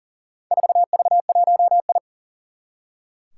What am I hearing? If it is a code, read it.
Morse code, heard as a keyed tone, about 40 wpm, 700 Hz. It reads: 4V1I
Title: Four Challenging Letter-Number Combinations 40wpm